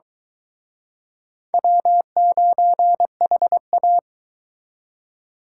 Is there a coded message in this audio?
Yes: Morse code from 1.5 to 4.0 s, heard as a keyed tone, about 23 wpm, 700 Hz. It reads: W9HA